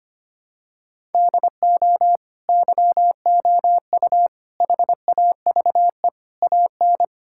Morse code, heard as a keyed tone, about 25 wpm, 700 Hz. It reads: DO YOU HAVE AN
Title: Sentences from Top 1000 Words - Review Part 5 - 25wpm